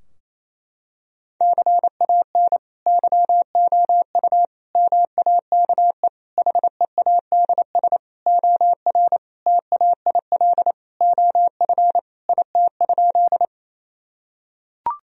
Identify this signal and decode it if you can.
Morse code, heard as a keyed tone, about 28 wpm, 700 Hz. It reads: CAN YOU MAKE HEADS OR TAIL OF IT?
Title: Sentences from Top 1000 Words - Review Part 3 - 28wpm